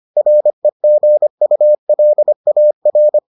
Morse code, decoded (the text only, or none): REGULAR